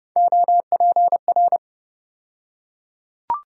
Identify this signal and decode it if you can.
Morse code, heard as a keyed tone, about 30 wpm, 700 Hz. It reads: OPR
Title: USA Call Sign Postfixes - 3 Letters 30wpm